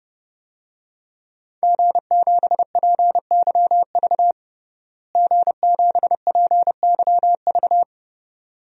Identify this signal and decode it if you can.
Morse code, heard as a keyed tone, about 30 wpm, 700 Hz. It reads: G7PYV G7PYV
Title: DXCC-100 Call Signs - 5 Characters - Repeated 2x - 30wpm